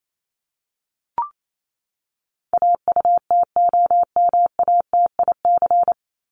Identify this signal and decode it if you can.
Morse code, heard as a keyed tone, about 28 wpm, 700 Hz. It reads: AUTOMATIC
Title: Top 1751-2000 Words 28wpm